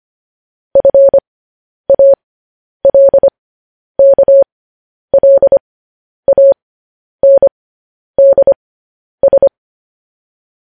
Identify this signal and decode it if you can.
Morse code, heard as a keyed tone, about 25 wpm, spaced as for 10 wpm, 550 Hz. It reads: FALKLANDS
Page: news headlines in morse code at 10 wpm